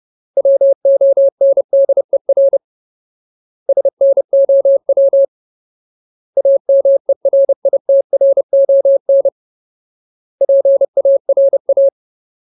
Morse code, heard as a keyed tone, about 30 wpm, 550 Hz. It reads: WONDER SNOW AMERITRON PARA
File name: Ultimate Word List / 30WPM Ultimate